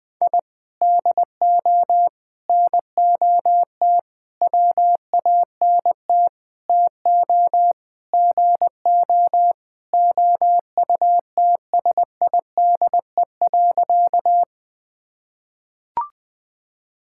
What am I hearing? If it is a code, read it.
Morse code, heard as a keyed tone, about 20 wpm, 700 Hz. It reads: I DO NOT WANT TO GO OUTSIDE.
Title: Sentences with 5-Word Phrases - Spoken Beginning - 20wpm